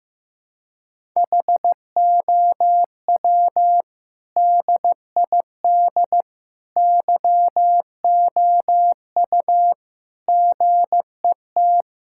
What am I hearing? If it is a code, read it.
Morse code, heard as a keyed tone, about 15 wpm, 700 Hz. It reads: HOW DID YOU GET
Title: Sentences from Top 1000 Words - Review Part 5 - 15wpm